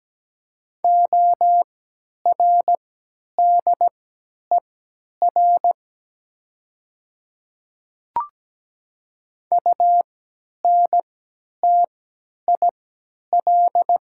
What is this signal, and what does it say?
Morse code, heard as a keyed tone, about 17 wpm, spaced as for 10 wpm, 700 Hz. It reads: ORDER UNTIL
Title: 3x Character Spacing - Top 500 Words - 17wpm